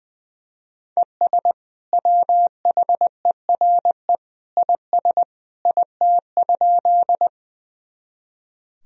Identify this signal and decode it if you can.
Morse code, heard as a keyed tone, about 20 wpm, 700 Hz. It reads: ES WHERE IS IT?
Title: Sentences from Top 200 Words and Abbreviations 20wpm